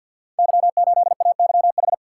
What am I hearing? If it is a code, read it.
Morse code, heard as a keyed tone, about 53 wpm, 700 Hz. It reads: Y8AYH
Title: Code Groups - 5 Letter:Numbers - 50wpm